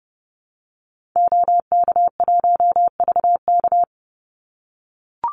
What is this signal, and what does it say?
Morse code, heard as a keyed tone, about 30 wpm, 700 Hz. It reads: OK1VK
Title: Call Signs - 30wpm